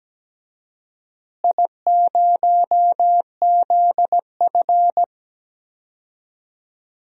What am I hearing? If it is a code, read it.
Morse code, heard as a keyed tone, about 17 wpm, 700 Hz. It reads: I0ZF
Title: DXCC-100 Call Signs - 4 Characters - Repeated 2x - 17wpm